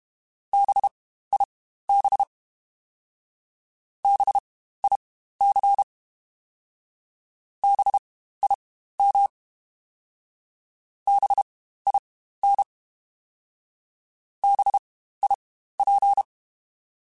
Morse code, heard as a keyed tone, about 32 wpm, spaced as for 15 wpm, 800 Hz. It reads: BIB BIC BIM BIN BIP
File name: Spaced Alphas / Spaced Alphas mp3s / Bs sp @30